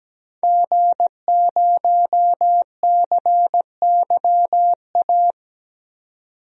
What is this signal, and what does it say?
Morse code, heard as a keyed tone, about 17 wpm, 700 Hz. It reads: G0CYA
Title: DXCC-100 Call Signs - 5 Characters - 17wpm